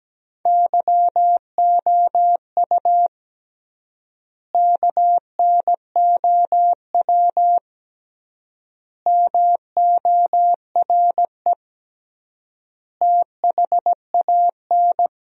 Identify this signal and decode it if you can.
Morse code, heard as a keyed tone, about 17 wpm, 700 Hz. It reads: YOU KNOW MORE THAN
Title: 3x Word-Spacing - Sentences from Top 100 Words - 17wpm